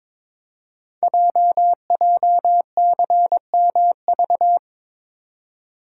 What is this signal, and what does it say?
Morse code, heard as a keyed tone, about 22 wpm, 700 Hz. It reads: JJCMV
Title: Code Groups - 5 Letters - 22wpm